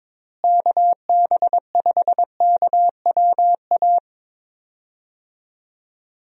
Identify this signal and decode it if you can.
Morse code, heard as a keyed tone, about 22 wpm, 700 Hz. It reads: KB5KWA
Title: USA Call Signs - 6 Characters - Repeated 3x - 22wpm